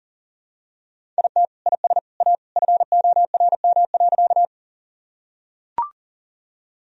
Morse code, heard as a keyed tone, about 40 wpm, 700 Hz. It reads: IT IS A FORM.
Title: Sentences from Top 1000 Words - Review Part 1 - 40wpm